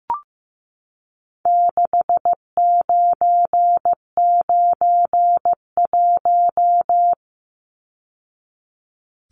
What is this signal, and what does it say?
Morse code, heard as a keyed tone, about 15 wpm, 700 Hz. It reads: 6991
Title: Numbers - 4-Digit 15wpm